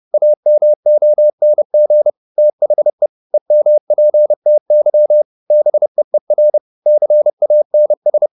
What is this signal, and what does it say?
Morse code, heard as a keyed tone, about 30 wpm, 600 Hz. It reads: AMONG THE EMPTY BEER CANS